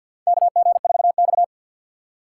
Morse code, heard as a keyed tone, about 50 wpm, 700 Hz. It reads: KG4X
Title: USA Call Signs - 6x Character Spacing - 50wpm